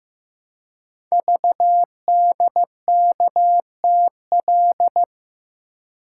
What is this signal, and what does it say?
Morse code, heard as a keyed tone, about 15 wpm, 700 Hz. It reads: VDKTL
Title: Code Groups - 5 Letters - 15wpm